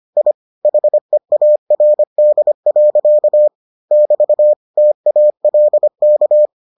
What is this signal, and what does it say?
Morse code, heard as a keyed tone, about 25 wpm, 600 Hz. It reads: I HEARD. <BT> TALK